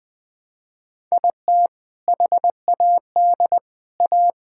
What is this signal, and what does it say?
Morse code, heard as a keyed tone, about 20 wpm, 700 Hz. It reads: IT HAD A
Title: Sentences from Top 1000 Words - Review Part 5 - 20wpm